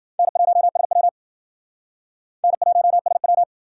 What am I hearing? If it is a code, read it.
Morse code, heard as a keyed tone, about 54 wpm, 700 Hz. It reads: N1SW N1SW